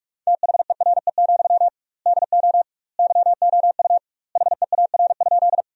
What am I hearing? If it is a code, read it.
Morse code, heard as a keyed tone, about 46 wpm, 700 Hz. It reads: THERE, DO YOU HEAR?
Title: 2x Word-Spacing - Sentences from Top 300 Words - 45wpm